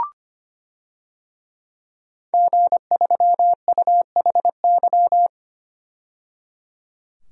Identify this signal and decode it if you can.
Morse code, heard as a keyed tone, about 25 wpm, 700 Hz. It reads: G3UHY
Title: DXCC-100 Call Signs - 5 Characters - 25wpm